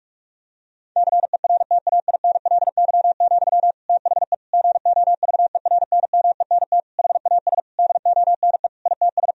Morse code, heard as a keyed tone, about 46 wpm, 700 Hz. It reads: CERTAINLY, THE GOVERNMENT HAS DONE ITS
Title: Sentences from Top 1000 Words - Review Part 3 - 45wpm